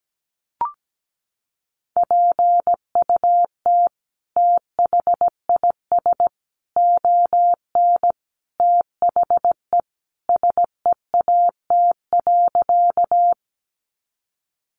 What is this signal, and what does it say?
Morse code, heard as a keyed tone, about 17 wpm, 700 Hz. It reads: PUT THIS ON THE SEAT.
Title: Sentences from Top 1500 Words 17wpm